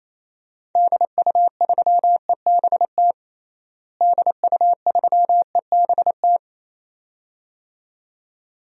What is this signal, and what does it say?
Morse code, heard as a keyed tone, about 28 wpm, 700 Hz. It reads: DU3EBT DU3EBT
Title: DXCC-100 Call Signs - 6 Characters - Repeated 2x - 28wpm